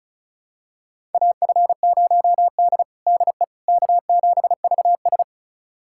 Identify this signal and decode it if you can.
Morse code, heard as a keyed tone, about 36 wpm, 700 Hz. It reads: AF0D DE K7VS
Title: USA Callsign DE Callsign - 4 Characters - 35wpm